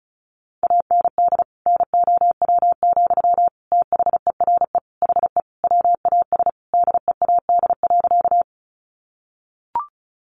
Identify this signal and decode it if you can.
Morse code, heard as a keyed tone, about 34 wpm, 700 Hz. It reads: AND NOW, THERE HE WAS DEAD.
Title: Sentences from Top 1500 Words 35wpm